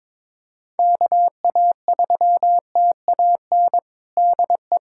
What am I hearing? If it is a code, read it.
Morse code, heard as a keyed tone, about 22 wpm, 700 Hz. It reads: KA3TAN DE
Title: USA Callsign DE Callsign - 6 Characters - 22wpm